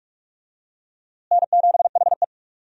Morse code, heard as a keyed tone, about 47 wpm, 700 Hz. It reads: N7HE
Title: USA Call Signs - 4 Characters - Repeated 2x - 45wpm